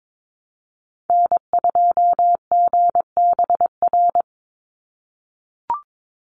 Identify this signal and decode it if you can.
Morse code, heard as a keyed tone, about 22 wpm, 700 Hz. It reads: N2GBR
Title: NA SOTA Activator Callsigns 22wpm